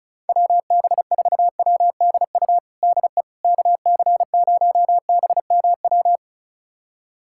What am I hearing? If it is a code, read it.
Morse code, heard as a keyed tone, about 34 wpm, 700 Hz. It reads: WB4WDU DE KC0BMW
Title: USA Callsign DE Callsign - 6 Characters - 35wpm